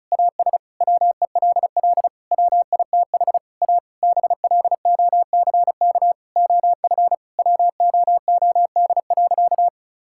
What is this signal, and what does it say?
Morse code, heard as a keyed tone, about 36 wpm, 700 Hz. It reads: AS WELL WITH A BLOCK OF WOOD.
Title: Sentences from Top 1600 Words 35wpm